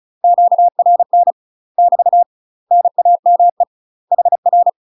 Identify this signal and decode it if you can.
Morse code, heard as a keyed tone, about 35 wpm, 700 Hz. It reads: QRN <BT> NAME HR
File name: UK MASTER QSOs TRAINING / 35WPM / UK MASTER QSOs 0013